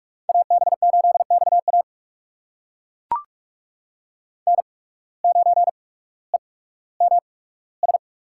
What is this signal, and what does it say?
Morse code, heard as a keyed tone, about 42 wpm, 690 Hz. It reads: A B 8 X A N 9 E M S
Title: USA Call Signs - 8x Character Spacing - 45wpm